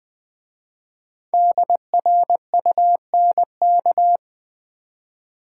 Morse code, heard as a keyed tone, about 20 wpm, 700 Hz. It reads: DRUNK
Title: 5 Letter Words - First 3 Letters Given - 20wpm